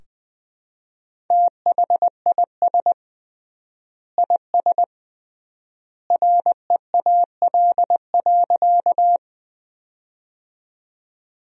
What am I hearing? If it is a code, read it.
Morse code, heard as a keyed tone, about 20 wpm, 700 Hz. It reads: THIS IS REAL.
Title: 3x Word-Spacing - Sentences from Top 300 Words - 20wpm